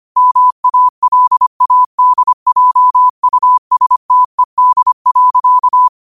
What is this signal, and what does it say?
Morse code, heard as a keyed tone, about 25 wpm, 1000 Hz. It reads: MALADJUSTED.
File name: qotdfiles / 25wpm